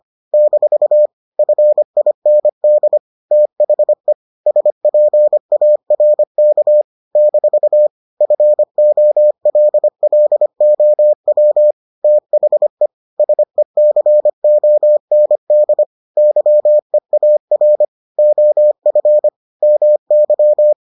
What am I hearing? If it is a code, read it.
Morse code, heard as a keyed tone, about 25 wpm, 600 Hz. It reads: - FIND THE SPARK - FOLLOW THE SECOND YEAR OF MY